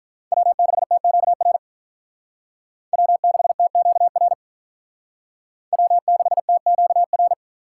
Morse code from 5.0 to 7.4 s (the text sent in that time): W6TQR